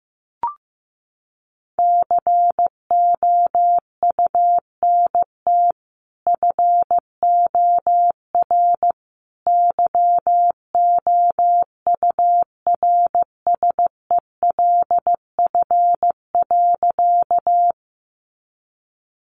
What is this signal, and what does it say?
Morse code, heard as a keyed tone, about 14 wpm, 700 Hz. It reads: COUNT FOR YOURSELF.
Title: Sentences from Top 1500 Words 15wpm